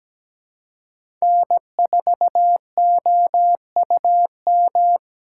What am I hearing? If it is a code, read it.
Morse code, heard as a keyed tone, about 17 wpm, 700 Hz. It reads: N4OUM
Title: USA Call Signs - 5 Characters - 6x Character Spacing - 17wpm